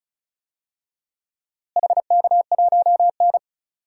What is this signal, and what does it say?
Morse code, heard as a keyed tone, about 36 wpm, 700 Hz. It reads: HK1N
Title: DXCC-100 Call Signs - 4 Characters - Repeated 2x - 35wpm